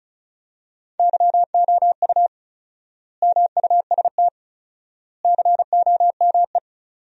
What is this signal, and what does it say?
Morse code, heard as a keyed tone, about 36 wpm, 700 Hz. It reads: YOU MUST COME
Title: 4x Word-Spacing - Sentences from Top 200 Words - 35wpm